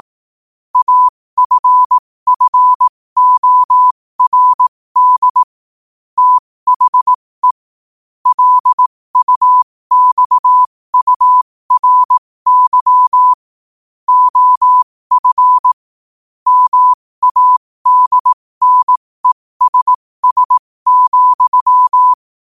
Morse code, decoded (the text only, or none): AFFORD THE LUXURY OF MADNESS,